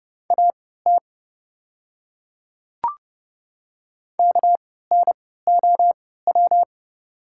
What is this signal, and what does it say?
Morse code, heard as a keyed tone, about 30 wpm, spaced as for 17 wpm, 690 Hz. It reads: AT KNOW
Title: 3x Character Spacing - Top 300 Words - 30wpm